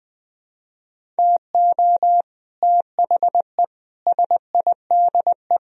Morse code, heard as a keyed tone, about 20 wpm, 700 Hz. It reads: TO THE SIDE